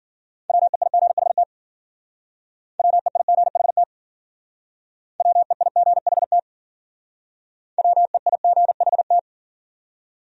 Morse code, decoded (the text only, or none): WEIGHT WEIGHT WEIGHT WEIGHT